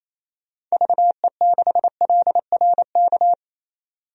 Morse code, heard as a keyed tone, about 28 wpm, 700 Hz. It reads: VE6LRK
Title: DXCC-100 Call Signs - 6 Characters - 28wpm